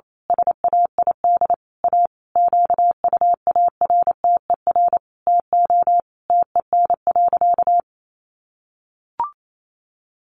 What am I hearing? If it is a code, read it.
Morse code, heard as a keyed tone, about 28 wpm, 700 Hz. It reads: SAID A QUARTER TO TEN.
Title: Sentences from Top 1300 Words 28wpm